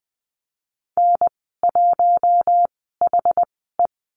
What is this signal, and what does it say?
Morse code, heard as a keyed tone, about 20 wpm, spaced as for 15 wpm, 700 Hz. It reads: N1HE
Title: 088 Lesson 15g - Number 5 - Call Signs 20wpm